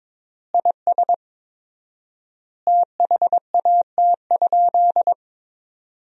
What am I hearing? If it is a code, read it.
Morse code, heard as a keyed tone, about 23 wpm, 700 Hz. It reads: IS THAT?
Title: 4x Word-Spacing - Sentences from Top 100 Words - 22wpm